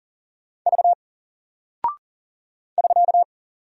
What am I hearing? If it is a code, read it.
Morse code, heard as a keyed tone, about 40 wpm, 700 Hz. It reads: V <SK>
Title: Single Character (Letter:Number:Punctuation:Prosign) Alt 40wpm